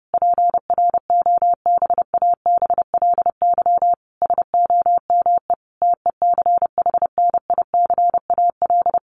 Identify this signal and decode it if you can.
Morse code, heard as a keyed tone, about 30 wpm, 700 Hz. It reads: PROBABLY SOME TECHNICAL
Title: Sentences from Top 1100 Words 30wpm